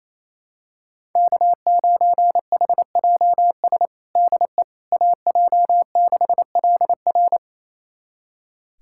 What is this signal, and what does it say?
Morse code, heard as a keyed tone, about 28 wpm, 700 Hz. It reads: K9HJS DE AJ6LR
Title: USA Callsign DE Callsign - 5 Characters - 28wpm